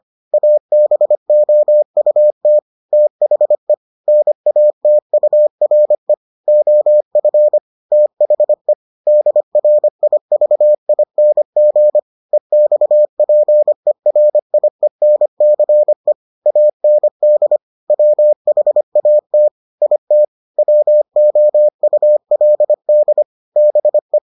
Morse code, heard as a keyed tone, about 25 wpm, 600 Hz. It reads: ABOUT THE NATURE OF THE DRIVING EXPERIENCE AND WHAT IT WOULD BE